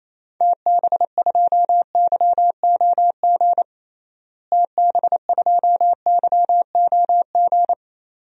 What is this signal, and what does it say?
Morse code, heard as a keyed tone, about 28 wpm, 700 Hz. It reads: TB2YOG TB2YOG
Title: DXCC-100 Call Signs - 6 Characters - Repeated 2x - 28wpm